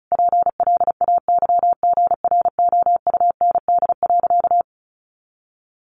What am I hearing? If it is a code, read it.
Morse code, heard as a keyed tone, about 34 wpm, 700 Hz. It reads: PLAYGROUND.
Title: Additional Easy Phrases 35wpm